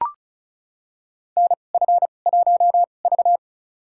Morse code, heard as a keyed tone, about 34 wpm, spaced as for 25 wpm, 700 Hz. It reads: NF1V
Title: USA Call Signs - 4 Characters - 2x Character Spacing - 35wpm